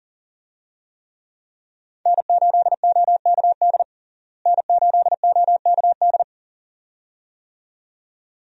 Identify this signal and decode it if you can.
Morse code, heard as a keyed tone, about 40 wpm, 700 Hz. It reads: N8OKD N8OKD
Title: USA Call Signs - 5 Characters - Repeated 2x - 40wpm